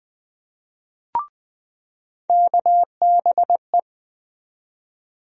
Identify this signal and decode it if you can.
Morse code, heard as a keyed tone, about 21 wpm, 700 Hz. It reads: KBE
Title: USA Call Sign Postfixes - 3 Letters 20wpm